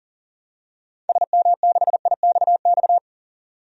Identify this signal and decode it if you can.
Morse code, heard as a keyed tone, about 40 wpm, 700 Hz. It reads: SM6IXX
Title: DXCC-100 Call Signs - 6 Characters - 40wpm